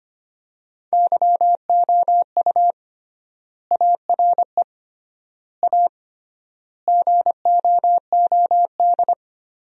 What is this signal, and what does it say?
Morse code, heard as a keyed tone, about 25 wpm, 700 Hz. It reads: YOU ARE A GOOD
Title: 3x Word-Spacing - Sentences from Top 200 Words - 25wpm